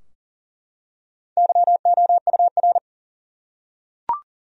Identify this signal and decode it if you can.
Morse code, heard as a keyed tone, about 37 wpm, 700 Hz. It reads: YOUR
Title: Top 100 Words Repeated 40wpm